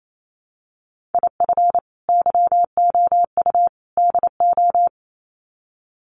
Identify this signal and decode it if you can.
Morse code, heard as a keyed tone, about 28 wpm, 700 Hz. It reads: IF YOU DO
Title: Sets of 3 Words 28wpm